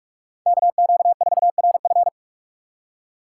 Morse code, heard as a keyed tone, about 43 wpm, 700 Hz. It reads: KQ4RF
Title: USA Call Signs - 4x Character Spacing - 45wpm